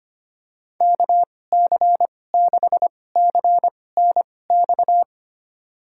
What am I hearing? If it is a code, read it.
Morse code, heard as a keyed tone, about 25 wpm, spaced as for 18 wpm, 700 Hz. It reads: KC6CNX
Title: USA Call Signs - 6 Characters - 2x Character Spacing - 25wpm